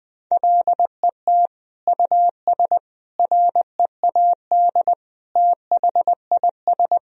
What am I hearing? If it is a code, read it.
Morse code, heard as a keyed tone, about 20 wpm, 700 Hz. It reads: LET US READ THIS
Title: Sentences from Top 1000 Words - Review Part 1 - 20wpm